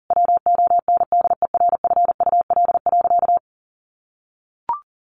Morse code, heard as a keyed tone, about 40 wpm, 700 Hz. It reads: WONDERFUL.
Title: Sentences from Top 1100 Words 40wpm